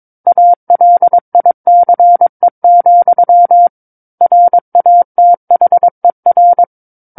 Morse code, heard as a keyed tone, about 22 wpm, 700 Hz. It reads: ALICE, RATHER
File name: 06-Chapter-06 22 WPM